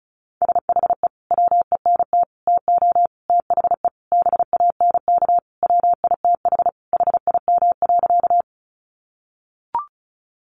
Morse code, heard as a keyed tone, about 34 wpm, 700 Hz. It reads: SHE WENT TO THE BANK WITH HIM.
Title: Sentences from Top 900 Words 35wpm